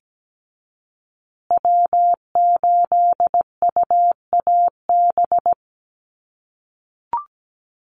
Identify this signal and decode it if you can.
Morse code, heard as a keyed tone, about 17 wpm, 700 Hz. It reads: W8UAB
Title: USA Call Signs Repeated 2x 17wpm